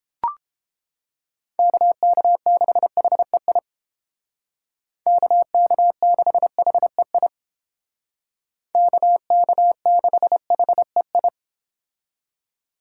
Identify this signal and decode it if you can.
Morse code, heard as a keyed tone, about 30 wpm, 700 Hz. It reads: KK6HEI KK6HEI KK6HEI